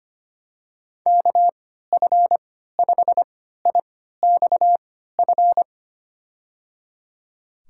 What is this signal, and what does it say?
Morse code, heard as a keyed tone, about 25 wpm, spaced as for 14 wpm, 700 Hz. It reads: KF5IXF
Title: USA Call Signs - 6 Characters - 3x Character Spacing - 25wpm